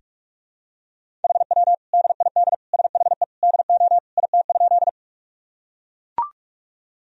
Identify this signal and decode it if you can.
Morse code, heard as a keyed tone, about 41 wpm, 700 Hz. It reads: HW DID SHE DO IT?
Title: Sentences from Top 100 Words and Abbreviations 45wpm